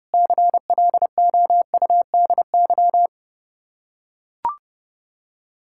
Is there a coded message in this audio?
Yes: Morse code from 0.1 to 3.1 s, heard as a keyed tone, about 30 wpm, 700 Hz. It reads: CLOUDY